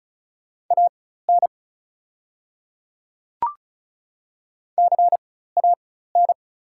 Morse code, heard as a keyed tone, about 35 wpm, spaced as for 16 wpm, 690 Hz. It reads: AN CAN
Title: 4x Character Spacing - Top 100 Words - 35wpm